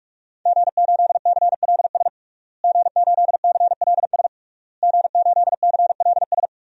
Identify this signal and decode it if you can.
Morse code, heard as a keyed tone, about 41 wpm, 700 Hz. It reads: G8CLS G8CLS G8CLS
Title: DXCC-100 Call Signs - 5 Characters - Repeated 3x - 45wpm